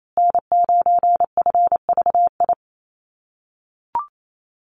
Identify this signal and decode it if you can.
Morse code, heard as a keyed tone, about 28 wpm, 700 Hz. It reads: N9FVI
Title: USA Call Signs Repeated 3x 28wpm